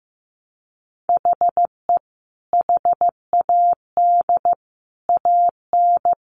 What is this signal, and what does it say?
Morse code, heard as a keyed tone, about 15 wpm, 700 Hz. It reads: HE HAD AN
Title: Sentences from Top 800 Words 15wpm